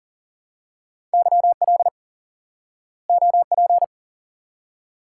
Morse code, heard as a keyed tone, about 38 wpm, 700 Hz. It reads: YL OP
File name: IQE Course 3 - Lesson 89 - YAGI - Two Elements - 40wpm